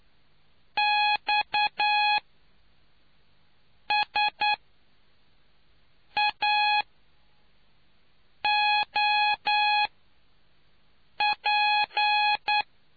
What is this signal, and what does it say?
Morse code, heard as a keyed tone, about 10 wpm, spaced as for 4 wpm, 840 Hz. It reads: XSAOP